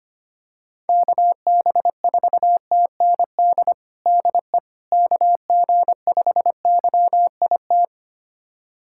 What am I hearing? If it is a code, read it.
Morse code, heard as a keyed tone, about 25 wpm, 700 Hz. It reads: KB4TND DE KG5YIT
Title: USA Callsign DE Callsign - 6 Characters - 25wpm